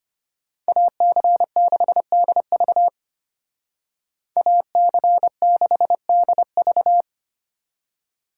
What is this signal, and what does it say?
Morse code, heard as a keyed tone, about 31 wpm, 700 Hz. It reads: AC6DV AC6DV
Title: Speed Racing - USA 2x2 Call Signs 25wpm